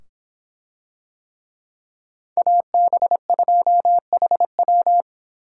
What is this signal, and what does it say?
Morse code, heard as a keyed tone, about 24 wpm, 700 Hz. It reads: AB2HW